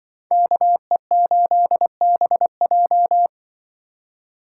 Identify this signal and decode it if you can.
Morse code, heard as a keyed tone, about 22 wpm, 700 Hz. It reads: KE8BJ
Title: Speed Racing - USA 2x2 Call Signs 20wpm